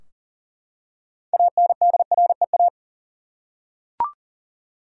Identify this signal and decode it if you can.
Morse code, heard as a keyed tone, about 40 wpm, 700 Hz. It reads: ANDREA
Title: USA Common Names - 40wpm